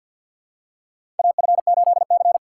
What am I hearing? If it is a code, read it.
Morse code, heard as a keyed tone, about 52 wpm, 700 Hz. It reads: AF8C
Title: USA Call Signs - 4 Characters - 6x Character Spacing - 50wpm